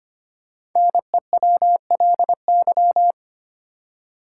Morse code, heard as a keyed tone, about 25 wpm, 700 Hz. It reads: NEWLY
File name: 5 Letter Words - First 3 Letters Given - 25wpm